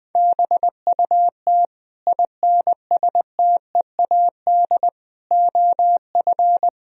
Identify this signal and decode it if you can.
Morse code, heard as a keyed tone, about 20 wpm, 700 Hz. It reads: BUT INSTEAD OF
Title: Sentences from Top 1000 Words - Review Part 4 - 20wpm